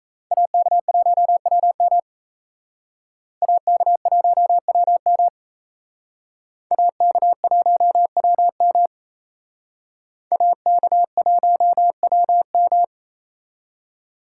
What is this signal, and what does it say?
Morse code, heard as a keyed tone, about 35 wpm, 700 Hz. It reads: AK1WM AK1WM AK1WM AK1WM
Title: Speed Racing - USA 2x2 Call Signs 28wpm